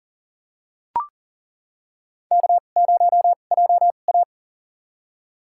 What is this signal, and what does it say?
Morse code, heard as a keyed tone, about 40 wpm, spaced as for 29 wpm, 700 Hz. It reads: K0JA
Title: USA Call Signs - 4 Characters - 2x Character Spacing - 40wpm